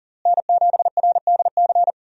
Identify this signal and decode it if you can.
Morse code, heard as a keyed tone, about 40 wpm, 700 Hz. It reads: N7RDC
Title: USA Call Signs - 5 Characters - Repeated 2x - 40wpm